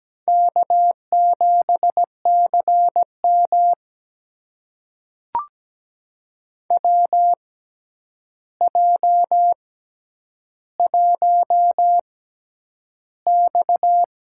Morse code, heard as a keyed tone, about 17 wpm, spaced as for 9 wpm, 700 Hz. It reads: K7CM WJ1X
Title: USA Call Signs - 4 Characters - 6x Character Spacing - 17wpm